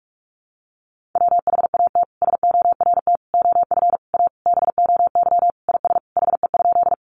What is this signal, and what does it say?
Morse code, heard as a keyed tone, about 44 wpm, 700 Hz. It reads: WHAT SORT OF A BOY IS HE?
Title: Sentences from Top 1300 Words 45wpm